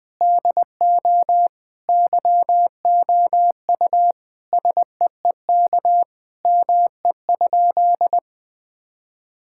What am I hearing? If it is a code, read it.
Morse code, heard as a keyed tone, about 20 wpm, 700 Hz. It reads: DO YOU SEEK ME?
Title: Sentences from Top 1600 Words 20wpm